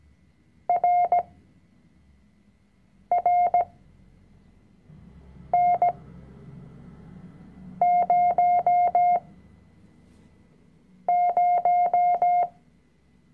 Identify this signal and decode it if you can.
Morse code, heard as a keyed tone, about 17 wpm, spaced as for 4 wpm, 710 Hz. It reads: RRN00
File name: CW-Online-Kurs / Anfänger / Lektion 005